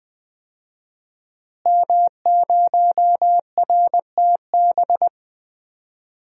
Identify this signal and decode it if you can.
Morse code, heard as a keyed tone, about 21 wpm, 700 Hz. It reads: M0RTB
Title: DXCC-100 Call Signs - 5 Characters - Repeated 2x - 20wpm